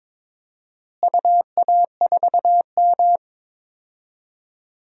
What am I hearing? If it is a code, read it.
Morse code, heard as a keyed tone, about 22 wpm, 700 Hz. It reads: UA4M
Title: DXCC-100 Call Signs - 4 Characters - Repeated 2x - 22wpm